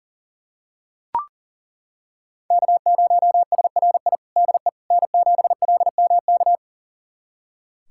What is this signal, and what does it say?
Morse code, heard as a keyed tone, about 40 wpm, 700 Hz. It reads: K0SRI DE N7LMK
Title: USA Callsign DE Callsign - 5 Characters - 40wpm